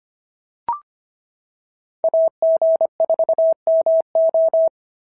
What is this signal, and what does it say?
Morse code, heard as a keyed tone, about 23 wpm, 650 Hz. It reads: AG4MO
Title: SOTA QSO Practice 25wpm